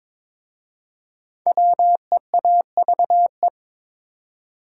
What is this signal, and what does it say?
Morse code, heard as a keyed tone, about 23 wpm, 700 Hz. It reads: WEAVE
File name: 5 Letter Words - First 3 Letters Given - 22wpm